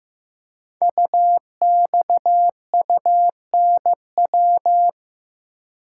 Morse code, heard as a keyed tone, about 15 wpm, 700 Hz. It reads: UXUNW
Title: Code Groups - 5 Letters - 15wpm